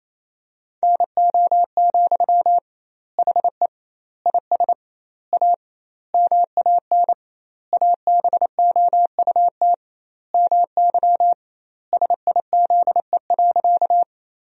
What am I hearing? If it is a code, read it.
Morse code, heard as a keyed tone, about 28 wpm, 700 Hz. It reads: NO, HE IS A MAN ABOUT MY SIZE.
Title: 2x Word-Spacing - Sentences from Top 400 Words - 28wpm